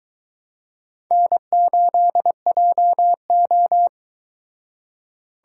Morse code, heard as a keyed tone, about 22 wpm, 700 Hz. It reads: N8JO